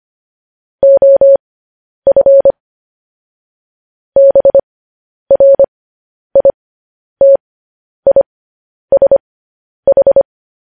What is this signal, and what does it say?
Morse code, heard as a keyed tone, about 25 wpm, spaced as for 10 wpm, 550 Hz. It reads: OF BRITISH
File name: podcast / News headlines-1x-25x10